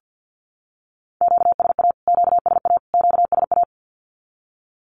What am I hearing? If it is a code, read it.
Morse code, heard as a keyed tone, about 48 wpm, 700 Hz. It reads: QSA QSA QSA
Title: Q-Codes Repeated 3x 50wpm